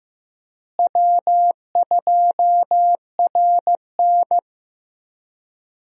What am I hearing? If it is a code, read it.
Morse code, heard as a keyed tone, about 15 wpm, 700 Hz. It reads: W2RN
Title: USA Call Signs - 4 Characters - Repeated 3x - 15wpm